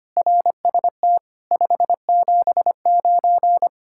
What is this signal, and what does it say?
Morse code, heard as a keyed tone, about 31 wpm, 700 Hz. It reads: RST 579
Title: Speed-Racing Common QSO Phrases - 25wpm